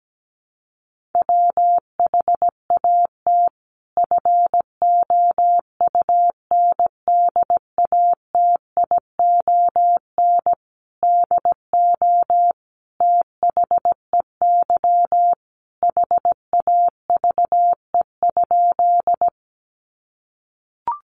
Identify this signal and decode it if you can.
Morse code, heard as a keyed tone, about 17 wpm, 700 Hz. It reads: WHAT FOUNDATION DO THEY HAVE?
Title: Sentences from Top 1200 Words 17wpm